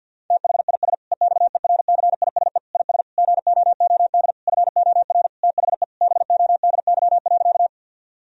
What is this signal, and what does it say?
Morse code, heard as a keyed tone, about 50 wpm, 700 Hz. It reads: THIS EXERCISE IS GOOD FOR THE BODY.
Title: Sentences from Top 1000 Words - Review Part 4 - 50wpm